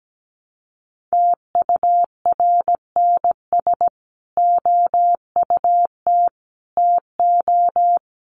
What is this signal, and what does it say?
Morse code, heard as a keyed tone, about 17 wpm, 700 Hz. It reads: TURNS OUT TO
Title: Sets of 5 Words Encore 17wpm